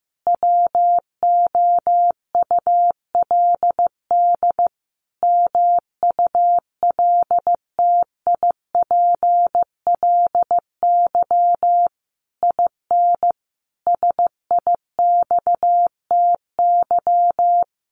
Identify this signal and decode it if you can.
Morse code, heard as a keyed tone, about 15 wpm, 700 Hz. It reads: WOULD MULTIPLY IN SIXTY
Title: A Sound Of Thunder 15wpm